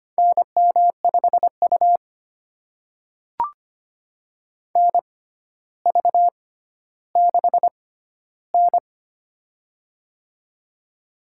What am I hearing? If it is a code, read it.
Morse code, heard as a keyed tone, about 25 wpm, spaced as for 13 wpm, 700 Hz. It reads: NM5U NV6N
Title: USA Call Signs - 4 Characters - 6x Character Spacing - 25wpm